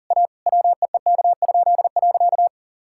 Morse code, heard as a keyed tone, about 40 wpm, 700 Hz. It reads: A WEEK?.
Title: Pangram Challenge v2 - 40wpm